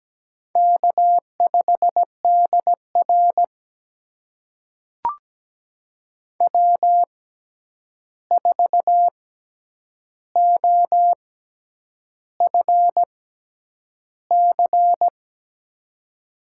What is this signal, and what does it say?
Morse code, heard as a keyed tone, about 17 wpm, spaced as for 8 wpm, 700 Hz. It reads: K5DR W4OFC
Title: USA Call Signs - 6x Character Spacing - 17wpm